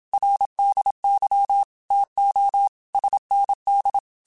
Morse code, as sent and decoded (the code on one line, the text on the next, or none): .-. -.. -.-- / - --- / ... -. -..
RDY TO SND